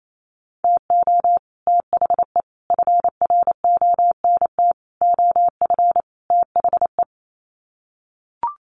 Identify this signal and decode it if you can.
Morse code, heard as a keyed tone, about 28 wpm, 700 Hz. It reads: TO THE FRONT OF THE
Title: Sets of 5 Words 28wpm